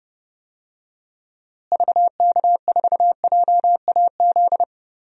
Morse code, heard as a keyed tone, about 30 wpm, 700 Hz. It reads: VK4JAZ
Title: DXCC-100 Call Signs - 6 Characters - Repeated 3x - 30wpm